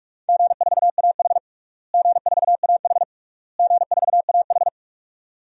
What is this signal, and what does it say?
Morse code, heard as a keyed tone, about 46 wpm, 700 Hz. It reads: G4AH G4AH G4AH
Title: DXCC-100 Call Signs - 4 Characters - Repeated 3x - 45wpm